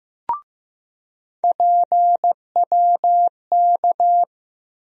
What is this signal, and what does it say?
Morse code, heard as a keyed tone, about 15 wpm, 700 Hz. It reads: PWK
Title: USA Call Sign Postfixes - 3 Letters 15wpm